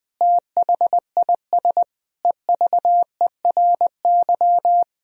Morse code, heard as a keyed tone, about 20 wpm, 700 Hz. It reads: THIS EVERY
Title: Sentences from Top 1000 Words - Review Part 3 - 20wpm